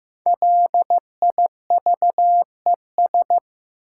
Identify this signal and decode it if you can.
Morse code, heard as a keyed tone, about 15 wpm, 700 Hz. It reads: LIVES
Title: Sets of 5 Words Encore - New Word Drill - 15wpm